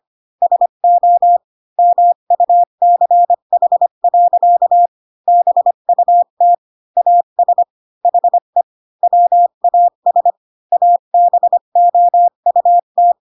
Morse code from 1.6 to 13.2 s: MUCH. BUT AS HE WAS ABOUT